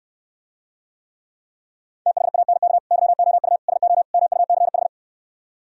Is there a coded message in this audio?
Yes: Morse code from 2.1 to 4.9 s, heard as a keyed tone, about 67 wpm, 700 Hz. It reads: THANK YOU SO MUCH